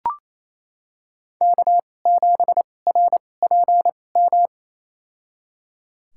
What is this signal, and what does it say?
Morse code, heard as a keyed tone, about 28 wpm, spaced as for 20 wpm, 700 Hz. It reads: K7RPM
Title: USA Call Signs - 5 Characters - 2x Character Spacing - 28wpm